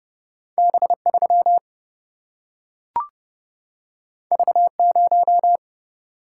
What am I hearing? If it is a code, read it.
Morse code, heard as a keyed tone, about 30 wpm, 700 Hz. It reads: B3 V0
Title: Single-Letter Single-Number 30wpm